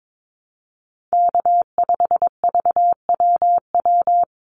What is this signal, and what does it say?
Morse code, heard as a keyed tone, about 22 wpm, 700 Hz. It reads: K5VWW
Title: Call Signs Repeated 2x - 22wpm